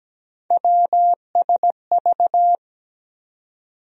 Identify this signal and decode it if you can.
Morse code, heard as a keyed tone, about 17 wpm, 700 Hz. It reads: WSV
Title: USA Call Sign Postfixes - 3 Letters 17wpm